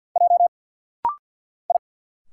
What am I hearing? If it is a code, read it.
Morse code, heard as a keyed tone, about 50 wpm, 700 Hz. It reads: J I
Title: Single Letter-Number Alt - Rapid-Fire 50wpm